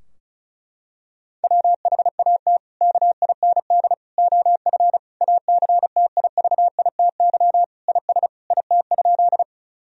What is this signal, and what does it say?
Morse code, heard as a keyed tone, about 36 wpm, 700 Hz. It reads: WHAT KIND OF ACTIVITY IS IT?
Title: Sentences from Top 1000 Words - Review Part 4 - 35wpm